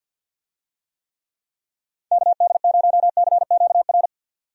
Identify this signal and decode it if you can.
Morse code, heard as a keyed tone, about 45 wpm, 700 Hz. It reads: KD0CQR
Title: USA Call Signs - 6 Characters - 50wpm